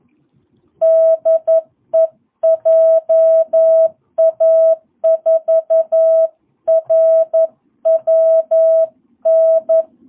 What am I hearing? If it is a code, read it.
Morse code, heard as a keyed tone, about 11 wpm, 660 Hz. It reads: DE JA4RWN
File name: jh8cbh / 2022cw